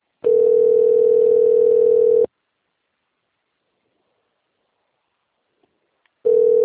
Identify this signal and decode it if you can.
Ringback tone (North America), 440 + 480 Hz.
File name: rg-1600-unknown- - - .16414